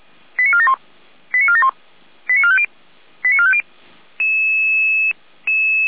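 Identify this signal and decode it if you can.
Five-tone selective call (ZVEI), heard as quick runs of five tones, tones of about 70 ms. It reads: 89471 89471 89450 89450
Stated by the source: Five-tone ZVEI selective call received on an SDR.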